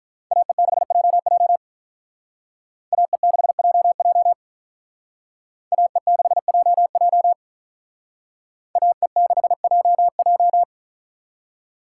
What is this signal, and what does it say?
Morse code, heard as a keyed tone, about 45 wpm, 700 Hz. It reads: AE6JJ AE6JJ AE6JJ AE6JJ